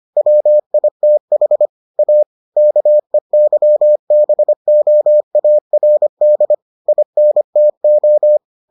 Morse code, heard as a keyed tone, about 25 wpm, 600 Hz. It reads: WITH A KEYBOARD INTO